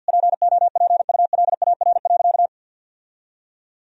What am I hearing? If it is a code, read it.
Morse code, heard as a keyed tone, about 50 wpm, 700 Hz. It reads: POPULAR.
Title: Sentences from Top 1000 Words - Review Part 3 - 50wpm